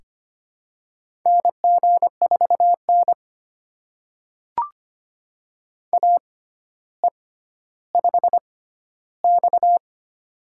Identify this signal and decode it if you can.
Morse code, heard as a keyed tone, about 25 wpm, spaced as for 13 wpm, 700 Hz. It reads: NG4N AE5X
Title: USA Call Signs - 4 Characters - 6x Character Spacing - 25wpm